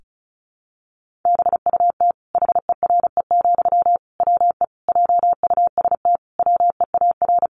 Morse code, heard as a keyed tone, about 34 wpm, 700 Hz. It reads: BUT HERE, WE JUST WEAR
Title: Sentences from Top 1500 Words 35wpm